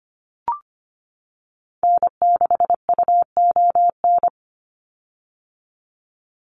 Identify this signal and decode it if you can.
Morse code, heard as a keyed tone, about 25 wpm, 700 Hz. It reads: N6UON
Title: USA Call Signs 25wpm